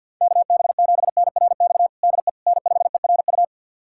Morse code, heard as a keyed tone, about 47 wpm, 700 Hz. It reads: KB7NRX DE N5ERU
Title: USA Callsign DE Callsign 50wpm